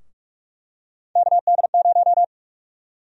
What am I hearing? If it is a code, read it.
Morse code, heard as a keyed tone, about 43 wpm, 700 Hz. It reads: KD0
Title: USA Call Sign Prefixes - Repeated 2x - 45wpm